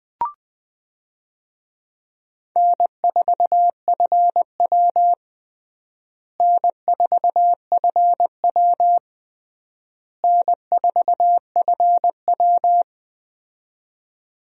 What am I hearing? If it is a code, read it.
Morse code, heard as a keyed tone, about 20 wpm, 700 Hz. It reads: N4FW N4FW N4FW
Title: USA Call Signs - 4 Characters - Repeated 3x - 20wpm